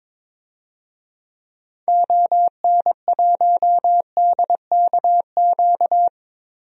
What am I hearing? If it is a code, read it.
Morse code, heard as a keyed tone, about 22 wpm, 700 Hz. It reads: ON1DKQ
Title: DXCC-100 Call Signs - 6 Characters - 22wpm